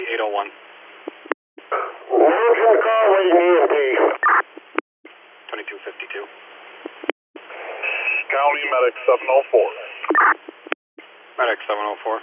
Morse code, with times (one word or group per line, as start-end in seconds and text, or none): none